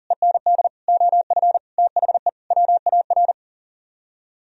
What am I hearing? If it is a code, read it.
Morse code, heard as a keyed tone, about 50 wpm, 700 Hz. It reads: END OF THE WAR